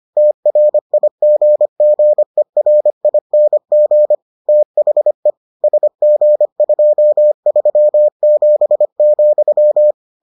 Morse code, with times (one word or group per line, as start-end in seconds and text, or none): 0.2-4.1 TRIGGERING
4.5-5.3 THE
5.6-9.9 SG237,